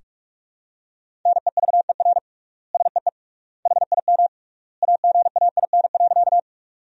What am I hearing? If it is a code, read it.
Morse code, heard as a keyed tone, about 43 wpm, 700 Hz. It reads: NEVER SEE HIM AGAIN.
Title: 3x Word-Spacing - Sentences from Top 300 Words - 45wpm